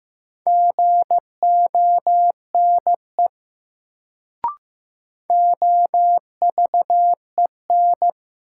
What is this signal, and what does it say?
Morse code, heard as a keyed tone, about 15 wpm, 700 Hz. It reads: GONE OVEN
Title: Sentences from Top 1200 Words - New Word Drill Alt - 15wpm